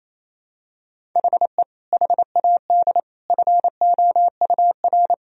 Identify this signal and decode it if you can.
Morse code, heard as a keyed tone, about 28 wpm, 700 Hz. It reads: HE HAD FOUR
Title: Sentences from Top 1000 Words - Review Part 1 - 28wpm